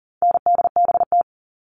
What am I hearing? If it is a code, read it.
Morse code, heard as a keyed tone, about 40 wpm, 700 Hz. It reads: NDBT
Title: All Cut Numbers - 4-Digit 40wpm